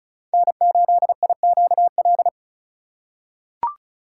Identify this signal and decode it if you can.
Morse code, heard as a keyed tone, about 36 wpm, 700 Hz. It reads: N8IQL
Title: USA Call Signs - 2x Character Spacing - 35wpm